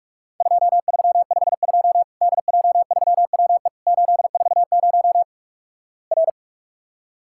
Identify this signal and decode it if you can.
Morse code, heard as a keyed tone, about 43 wpm, 700 Hz. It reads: 2352 DJ3WE 840 R
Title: Worked All Europe DX Contest - QTC Exchange - 45wpm